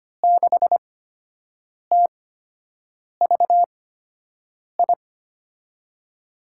Morse code, heard as a keyed tone, about 25 wpm, spaced as for 7 wpm, 700 Hz. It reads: 6TVI
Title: USA Call Signs - 6 Characters - 8x Character Spacing - 25wpm